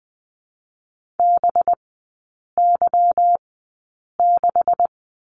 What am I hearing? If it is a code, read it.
Morse code, heard as a keyed tone, about 20 wpm, spaced as for 8 wpm, 700 Hz. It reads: BY6
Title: 184 Lesson 30e - Letter b - Three Characters 20wpm